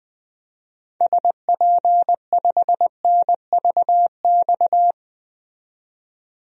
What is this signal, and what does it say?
Morse code, heard as a keyed tone, about 20 wpm, 700 Hz. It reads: SP5NVX
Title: DXCC-100 Call Signs - 6 Characters - 20wpm